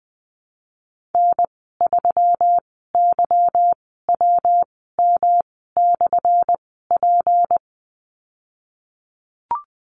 Morse code, heard as a keyed tone, about 20 wpm, spaced as for 15 wpm, 700 Hz. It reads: N3YWM/P
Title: 179 Lesson 29g - Slash - Call Signs 20wpm